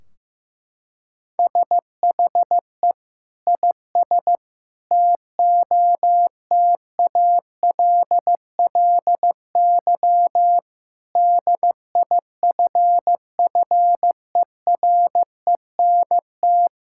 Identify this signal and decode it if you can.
Morse code, heard as a keyed tone, about 15 wpm, 700 Hz. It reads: SHE IS TOTALLY DIFFERENT
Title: Sentences from Top 1600 Words 15wpm